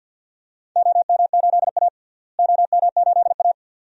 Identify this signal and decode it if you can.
Morse code, heard as a keyed tone, about 51 wpm, 700 Hz. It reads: OM8A OM8A
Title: DXCC-100 Call Signs - 4 Characters - Repeated 2x - 50wpm